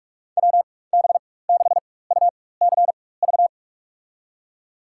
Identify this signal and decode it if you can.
Morse code, heard as a keyed tone, about 46 wpm, spaced as for 21 wpm, 700 Hz. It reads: WB6UCV
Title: USA Call Signs - 6 Characters - 4x Character Spacing - 45wpm